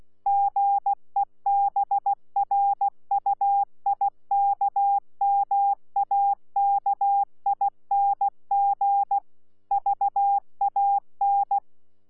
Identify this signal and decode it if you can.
Morse code, heard as a keyed tone, about 16 wpm, 800 Hz. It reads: GEBRUIKMAKING VAN
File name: Z PI4VRZ / PI4VRZ 14-jun-2025